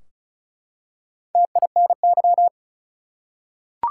Morse code, heard as a keyed tone, about 36 wpm, 700 Hz. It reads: TINY
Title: Four Letter Words - 35wpm